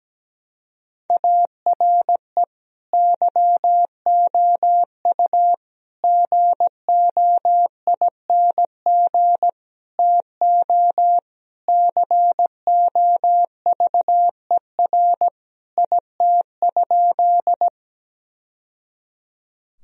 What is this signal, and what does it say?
Morse code, heard as a keyed tone, about 17 wpm, 700 Hz. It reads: ARE YOU GOING TO COVER IT?
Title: Sentences from Top 1000 Words - Review Part 1 - 17wpm